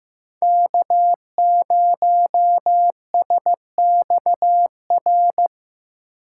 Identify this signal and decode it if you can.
Morse code, heard as a keyed tone, about 15 wpm, 700 Hz. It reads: K 0 S X R
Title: USA Call Signs - 8x Character Spacing - 15wpm